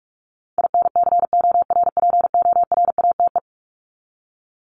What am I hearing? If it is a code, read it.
Morse code, heard as a keyed tone, about 44 wpm, 700 Hz. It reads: INCORPORATE
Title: Top 1751-2000 Words 45wpm